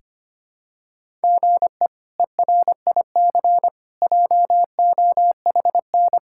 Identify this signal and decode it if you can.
Morse code, heard as a keyed tone, about 25 wpm, 700 Hz. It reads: GE ERIC JOHN
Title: K1USN SST QSO Practice - 25wpm